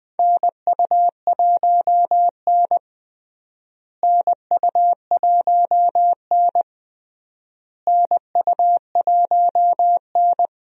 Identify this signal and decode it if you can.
Morse code, heard as a keyed tone, about 20 wpm, 700 Hz. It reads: NU1N NU1N NU1N
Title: USA Call Signs - 4 Characters - Repeated 3x - 20wpm